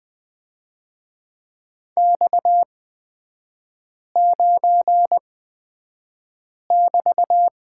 Morse code, heard as a keyed tone, about 20 wpm, spaced as for 5 wpm, 700 Hz. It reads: X9<BT>
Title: 243 Lesson 39e - Letter z - Three Characters 20wpm